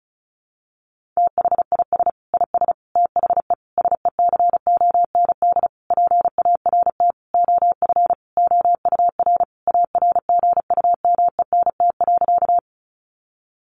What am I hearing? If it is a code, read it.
Morse code, heard as a keyed tone, about 34 wpm, 700 Hz. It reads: THIS IS THE SECOND PART OF OUR ARGUMENT.
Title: Sentences from Top 1400 Words 35wpm